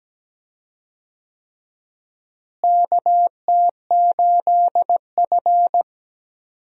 Morse code, heard as a keyed tone, about 17 wpm, 700 Hz. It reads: KT8F
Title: USA Call Signs - 4 Characters - Repeated 3x - 17wpm